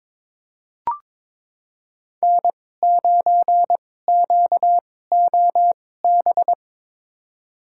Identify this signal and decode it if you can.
Morse code, heard as a keyed tone, about 22 wpm, spaced as for 16 wpm, 700 Hz. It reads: N9QOB
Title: USA Call Signs - 2x Character Spacing - 22wpm